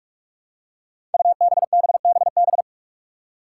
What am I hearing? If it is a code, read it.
Morse code, heard as a keyed tone, about 44 wpm, 700 Hz. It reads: UBBBB
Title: Five Challenging Letter-Number Combinations 45wpm